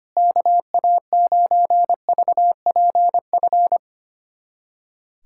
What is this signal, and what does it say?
Morse code, heard as a keyed tone, about 25 wpm, 700 Hz. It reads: KA9VPF
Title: USA Callsign DE Callsign - 6 Characters - 25wpm